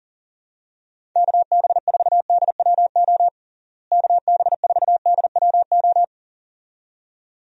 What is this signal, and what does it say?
Morse code, heard as a keyed tone, about 40 wpm, 700 Hz. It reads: KB4DWO KB4DWO
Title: USA Call Signs - 6 Characters - Repeated 2x - 40wpm